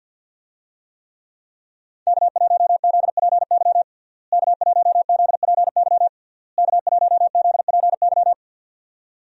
Morse code, heard as a keyed tone, about 51 wpm, 700 Hz. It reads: K1ZPY K1ZPY K1ZPY
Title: USA Call Signs - 5 Characters - Repeated 3x - 50wpm